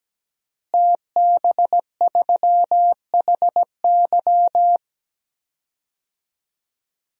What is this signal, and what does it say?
Morse code, heard as a keyed tone, about 17 wpm, 700 Hz. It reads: TB3HY
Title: DXCC-100 Call Signs - 5 Characters - Repeated 3x - 17wpm